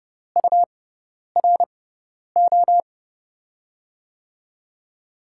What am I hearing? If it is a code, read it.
Morse code, heard as a keyed tone, about 32 wpm, spaced as for 10 wpm, 700 Hz. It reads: URO
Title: USA Call Signs - 5 Characters - 6x Character Spacing - 30wpm